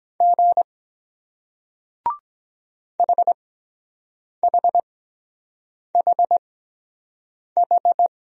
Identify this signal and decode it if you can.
Morse code, heard as a keyed tone, about 23 wpm, spaced as for 7 wpm, 700 Hz. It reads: G HHHH